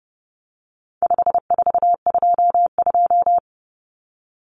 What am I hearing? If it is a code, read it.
Morse code, heard as a keyed tone, about 30 wpm, 700 Hz. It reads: 5422
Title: Numbers - 4-Digit 30wpm